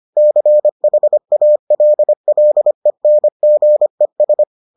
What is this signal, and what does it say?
Morse code, heard as a keyed tone, about 25 wpm, 600 Hz. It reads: CHALLENGES